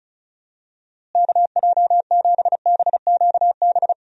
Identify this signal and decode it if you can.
Morse code, heard as a keyed tone, about 36 wpm, 700 Hz. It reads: KJ7BQB
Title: USA Call Signs - 6 Characters - Repeated 3x - 35wpm